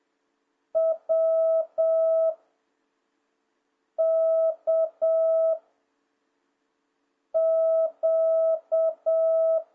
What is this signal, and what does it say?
Morse code, heard as a keyed tone, about 7 wpm, 660 Hz. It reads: W K Q